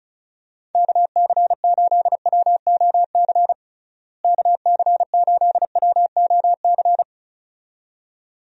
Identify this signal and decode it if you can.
Morse code, heard as a keyed tone, about 34 wpm, 700 Hz. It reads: KC8WOC KC8WOC
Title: USA Call Signs - 6 Characters - Repeated 2x - 35wpm